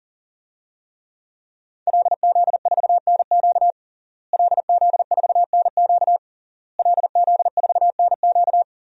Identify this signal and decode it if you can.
Morse code, heard as a keyed tone, about 40 wpm, 700 Hz. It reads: LZ4NQ LZ4NQ LZ4NQ
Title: DXCC-100 Call Signs - 5 Characters - Repeated 3x - 40wpm